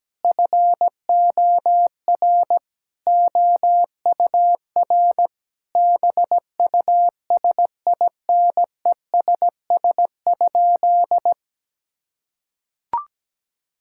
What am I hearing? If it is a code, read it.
Morse code, heard as a keyed tone, about 17 wpm, 700 Hz. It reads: FOR OUR BUSINESS?
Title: Sentences from Top 1000 Words - Review Part 5 - 17wpm